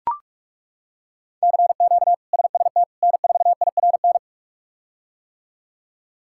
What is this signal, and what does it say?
Morse code, heard as a keyed tone, about 42 wpm, 700 Hz. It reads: CQ SST N4IRN
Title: K1USN SST QSO Practice - 45wpm